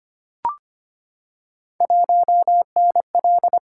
Morse code, heard as a keyed tone, about 25 wpm, 700 Hz. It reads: 1NL
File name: Three Letter-Number Combinations - 25wpm